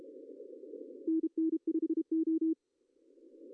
Morse code, heard as a keyed tone, about 30 wpm, 320 Hz. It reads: NN5O
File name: CW / cqww-cw-k5tr